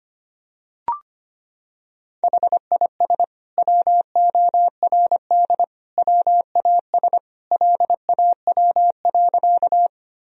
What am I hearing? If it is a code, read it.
Morse code, heard as a keyed tone, about 25 wpm, 700 Hz. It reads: HIS WORD WAS LAW.
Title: Sentences from Top 1000 Words - Review Part 3 - 25wpm